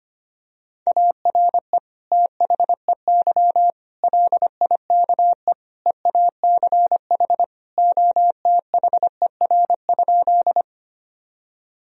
Morse code, heard as a keyed tone, about 25 wpm, 700 Hz. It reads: ARE THEY LIKE EACH OTHER?
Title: Sentences from Top 1000 Words - Review Part 1 - 25wpm